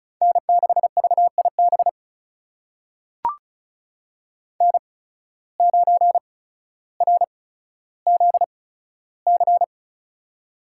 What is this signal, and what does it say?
Morse code, heard as a keyed tone, about 36 wpm, spaced as for 15 wpm, 700 Hz. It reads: N6VIB N9RZC
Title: USA Call Signs - 8x Character Spacing - 35wpm